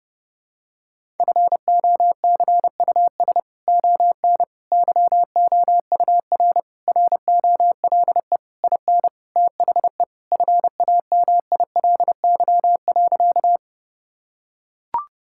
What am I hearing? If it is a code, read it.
Morse code, heard as a keyed tone, about 30 wpm, 700 Hz. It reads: FOCUS ON YOUR ROLE IN THE FAMILY.
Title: Sentences from Top 1000 Words - Review Part 6 - 30wpm